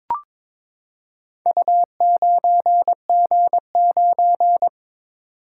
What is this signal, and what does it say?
Morse code, heard as a keyed tone, about 22 wpm, 700 Hz. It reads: U9G9
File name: practice / 22wpm / Four Letter-Number Combinations - 22wpm